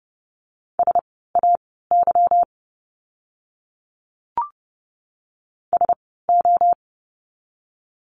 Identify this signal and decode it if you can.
Morse code, heard as a keyed tone, about 30 wpm, spaced as for 17 wpm, 700 Hz. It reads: SAY SO
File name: course / 30:10wpm / 230 Lesson 37f - Letter x - Words 30wpm